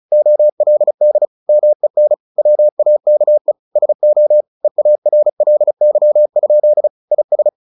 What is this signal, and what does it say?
Morse code, heard as a keyed tone, about 35 wpm, 600 Hz. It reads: OLD MEN WAKE SO EARLY? IS
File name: the old man and the sea / the old man and the sea 1 35wpm